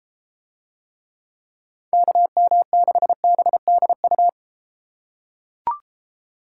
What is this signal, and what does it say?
Morse code, heard as a keyed tone, about 31 wpm, 700 Hz. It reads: KM6BDU